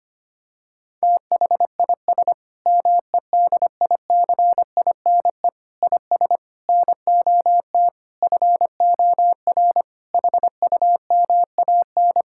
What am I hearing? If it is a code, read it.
Morse code, heard as a keyed tone, about 25 wpm, 700 Hz. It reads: THIS MEDICINE IS NOT FOR HUMAN
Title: Sentences from Top 1000 Words - Review Part 6 - 25wpm